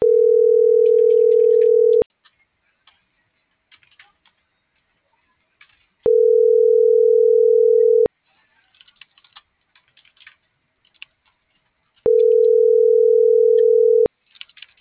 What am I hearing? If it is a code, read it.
Ringback tone (North America), 440 + 480 Hz.